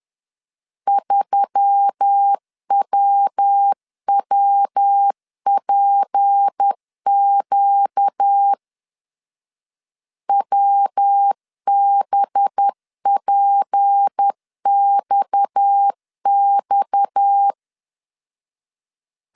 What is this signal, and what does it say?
Morse code, heard as a keyed tone, about 10 wpm, 800 Hz. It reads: 3WWPQ WBPXX
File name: CW / DET / 1 6 1500C S B 10wpm 4137